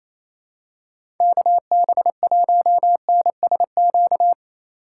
Morse code, heard as a keyed tone, about 27 wpm, 700 Hz. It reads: KB1NSQ
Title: USA Call Signs - 4x Character Spacing - 28wpm